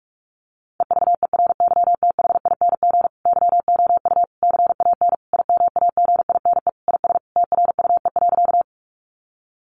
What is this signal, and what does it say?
Morse code, heard as a keyed tone, about 44 wpm, 700 Hz. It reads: EVERYTHING YOU CAN IMAGINE IS TRUE.
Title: Sentences from Top 1400 Words 45wpm